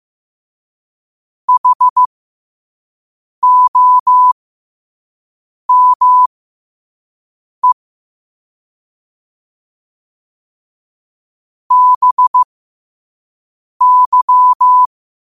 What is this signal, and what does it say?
Morse code, heard as a keyed tone, about 15 wpm, spaced as for 5 wpm, 1000 Hz. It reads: HOME BY
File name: qotdfiles / 5wpm